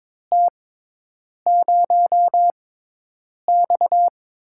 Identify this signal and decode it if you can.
Morse code, heard as a keyed tone, about 22 wpm, spaced as for 8 wpm, 700 Hz. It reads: T0X
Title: USA Call Signs - 4 Characters - 6x Character Spacing - 22wpm